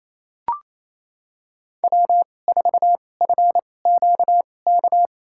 Morse code, heard as a keyed tone, about 28 wpm, spaced as for 20 wpm, 700 Hz. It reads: W4FQK
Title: USA Call Signs - 5 Characters - 2x Character Spacing - 28wpm